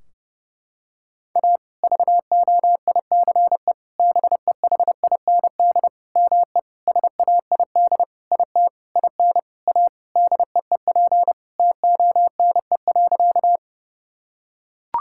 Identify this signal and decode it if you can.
Morse code, heard as a keyed tone, about 28 wpm, 700 Hz. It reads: A VOICE BEHIND ME SAID IT IN A DEEP TONE.
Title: Sentences from Top 1600 Words 30wpm